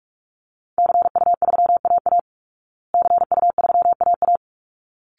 Morse code, heard as a keyed tone, about 44 wpm, 700 Hz. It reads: CU3AA CU3AA
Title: Call Signs Repeated 3x - 45wpm